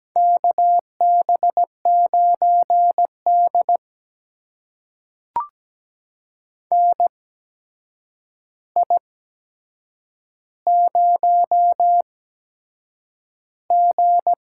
Morse code, heard as a keyed tone, about 17 wpm, spaced as for 7 wpm, 700 Hz. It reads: KB9D NI0G
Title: USA Call Signs - 4 Characters - 8x Character Spacing - 17wpm